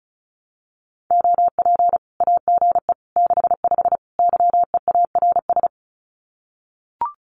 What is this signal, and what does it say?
Morse code, heard as a keyed tone, about 34 wpm, 700 Hz. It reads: OP AGE 65 YEARS
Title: Common QSO Phrases 35wpm